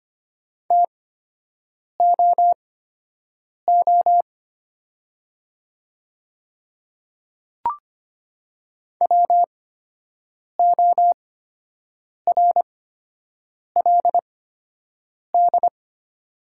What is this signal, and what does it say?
Morse code, heard as a keyed tone, about 25 wpm, spaced as for 7 wpm, 700 Hz. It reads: TOO WORLD
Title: 8x Character Spacing - Top 200 Words - 25wpm